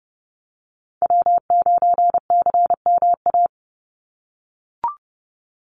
Morse code, heard as a keyed tone, about 30 wpm, 700 Hz. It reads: W9CMA
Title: USA Call Signs 30wpm